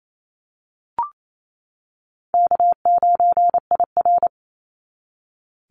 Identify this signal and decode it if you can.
Morse code, heard as a keyed tone, about 28 wpm, 700 Hz. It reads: K9IR
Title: NA SOTA Activator Callsigns 28wpm